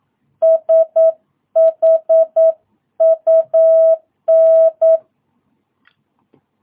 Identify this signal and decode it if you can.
Morse code, heard as a keyed tone, about 9 wpm, 660 Hz. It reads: SHUN
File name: jh8cbh / 2022cw